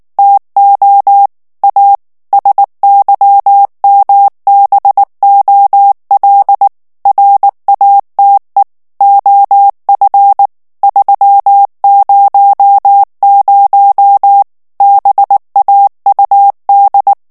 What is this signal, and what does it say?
Morse code, heard as a keyed tone, about 19 wpm, 800 Hz. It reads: TO A SYMBOL RATE OF 300 BAUD
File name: ANGLAIS texte clair 3 vitesse 18 mots minute